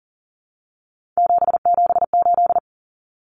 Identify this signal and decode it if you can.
Morse code, heard as a keyed tone, about 40 wpm, 700 Hz. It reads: 778
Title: Common Cut Numbers - 3-Digit 40wpm